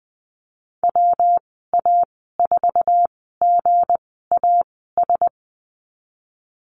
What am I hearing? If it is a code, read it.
Morse code, heard as a keyed tone, about 20 wpm, spaced as for 15 wpm, 700 Hz. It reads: WA4GAS
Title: 193 Lesson 31g - Letter v - Call Signs 20wpm